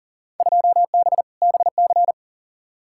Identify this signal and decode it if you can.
Morse code, heard as a keyed tone, about 40 wpm, 700 Hz. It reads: 2B BC
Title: Field Day Exchanges Encore - 40wpm